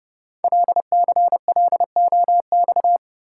Morse code, heard as a keyed tone, about 28 wpm, 700 Hz. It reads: LCLOX
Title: Code Groups - 5 Letter:Numbers - 30wpm